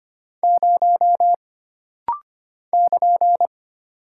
Morse code, heard as a keyed tone, about 25 wpm, 700 Hz. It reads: 0 <KN>
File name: Single Character (Letter:Number:Punctuation:Prosign) - Rapid-Fire - 25wpm